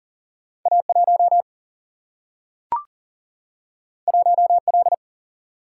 Morse code, heard as a keyed tone, about 40 wpm, 700 Hz. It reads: A 1 1 L
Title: Single-Letter Single-Number 40wpm